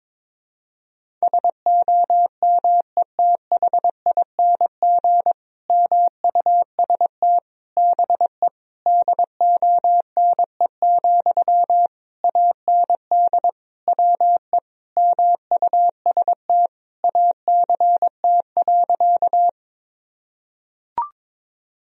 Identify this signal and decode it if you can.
Morse code, heard as a keyed tone, about 22 wpm, 700 Hz. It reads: SOMETHING MUST BE DONE, AND WE MUST ACT.
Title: Sentences from Top 1000 Words - Review Part 1 - 22wpm